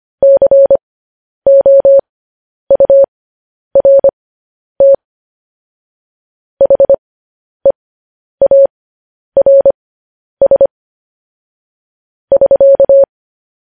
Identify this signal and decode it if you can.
Morse code, heard as a keyed tone, about 25 wpm, spaced as for 10 wpm, 550 Hz. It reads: COURT HEARS <SK>